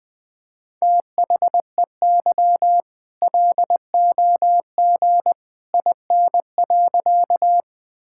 Morse code, heard as a keyed tone, about 21 wpm, 700 Hz. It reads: THEY LOG IN.
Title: All-Age Appropriate Jokes - 20wpm